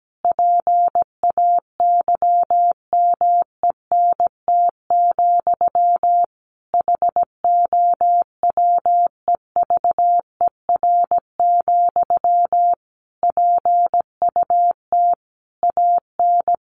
Morse code, heard as a keyed tone, about 17 wpm, 700 Hz. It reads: PAYMENT, HOWEVER, PUT AN
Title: Sentences from Top 1200 Words 17wpm